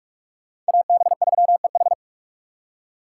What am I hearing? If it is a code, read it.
Morse code, heard as a keyed tone, about 58 wpm, 690 Hz. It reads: AB3EH